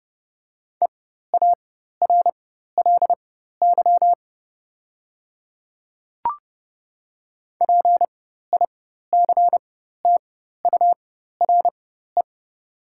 Morse code, heard as a keyed tone, about 30 wpm, spaced as for 14 wpm, 690 Hz. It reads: EARLY PICTURE
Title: 4x Character Spacing - Top 500 Words - 30wpm